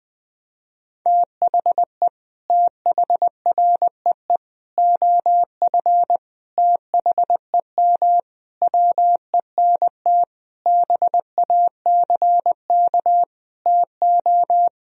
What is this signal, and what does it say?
Morse code, heard as a keyed tone, about 20 wpm, 700 Hz. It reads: THE THREE OF THEM WENT BACK TO
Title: Sentences from Top 1000 Words - Review Part 1 - 20wpm